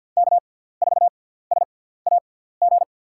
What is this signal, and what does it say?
Morse code, heard as a keyed tone, about 46 wpm, spaced as for 17 wpm, 700 Hz. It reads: K4SAG
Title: USA Call Signs - 5 Characters - 6x Character Spacing - 50wpm